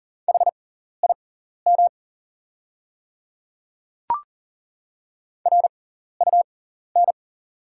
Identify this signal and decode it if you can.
Morse code, heard as a keyed tone, about 40 wpm, spaced as for 14 wpm, 700 Hz. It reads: HIM RUN
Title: 6x Character Spacing - Top 300 Words - 40wpm